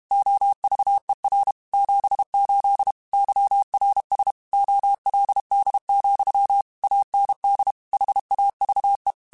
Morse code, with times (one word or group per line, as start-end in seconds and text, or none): 0.1-1.5 OVER
1.7-2.9 78
3.1-4.3 YRS
4.5-6.6 OLD,
6.8-7.7 AND
7.9-9.1 HAVE